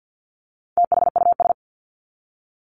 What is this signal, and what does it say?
Morse code, heard as a keyed tone, about 50 wpm, 700 Hz. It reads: THUS
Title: Top 751-1000 Words 50wpm